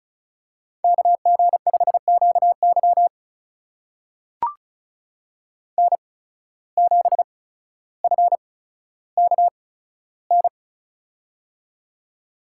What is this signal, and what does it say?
Morse code, heard as a keyed tone, about 36 wpm, spaced as for 15 wpm, 700 Hz. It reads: KG5QY N7FKN
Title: USA Call Signs - 5 Characters - 8x Character Spacing - 35wpm